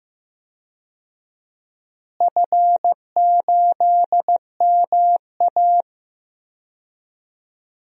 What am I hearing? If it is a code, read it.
Morse code, heard as a keyed tone, about 15 wpm, 700 Hz. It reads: F8MA
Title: DXCC-100 Call Signs - 4 Characters - Repeated 2x - 15wpm